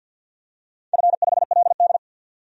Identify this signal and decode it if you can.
Morse code, heard as a keyed tone, about 53 wpm, 700 Hz. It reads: F5LD
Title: DXCC-100 Call Signs - 4 Characters - 50wpm